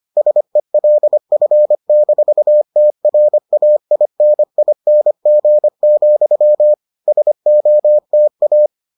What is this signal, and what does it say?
Morse code, heard as a keyed tone, about 25 wpm, 600 Hz. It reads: SELF-TRAINING, SOTA